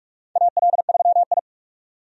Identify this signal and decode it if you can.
Morse code, heard as a keyed tone, about 43 wpm, 700 Hz. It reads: AL3I
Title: USA Call Signs - 4 Characters - 45wpm